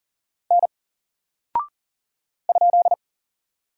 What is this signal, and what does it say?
Morse code, heard as a keyed tone, about 40 wpm, 700 Hz. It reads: N ?
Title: Single Character (Letter:Number:Punctuation:Prosign) Alt 40wpm